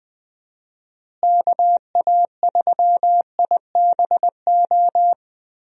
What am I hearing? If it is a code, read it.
Morse code, heard as a keyed tone, about 20 wpm, 700 Hz. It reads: KA3IBO
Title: USA Call Signs - 6 Characters - Repeated 2x - 20wpm